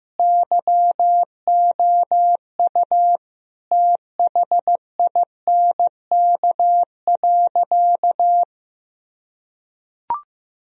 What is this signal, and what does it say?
Morse code, heard as a keyed tone, about 15 wpm, 700 Hz. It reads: YOU THINK.
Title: Sentences from Top 1000 Words - Review Part 2 - 15wpm